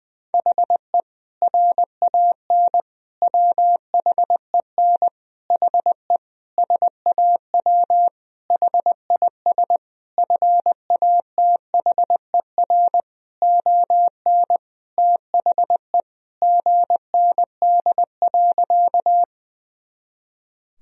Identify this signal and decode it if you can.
Morse code, heard as a keyed tone, about 20 wpm, 700 Hz. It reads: HE RAN WHEN HE SAW HIS FATHER ON THE GND.
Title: Sentences from Top 500 Words and Abbreviations 20wpm